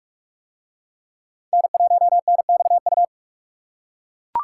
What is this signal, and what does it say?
Morse code, heard as a keyed tone, about 40 wpm, 700 Hz. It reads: N1NXU